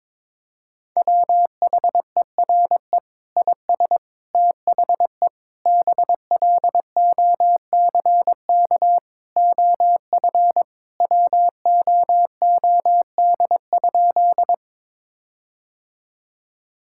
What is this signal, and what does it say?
Morse code, heard as a keyed tone, about 22 wpm, 700 Hz. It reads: WHERE IS THE BLOCK OF WOOD?
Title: Sentences from Top 1600 Words 22wpm